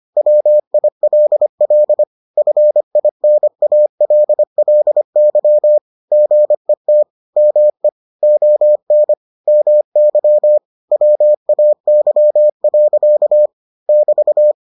Morse code, heard as a keyed tone, about 25 wpm, 600 Hz. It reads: WILL FINALLY GET ME ON MY WAY. <BT>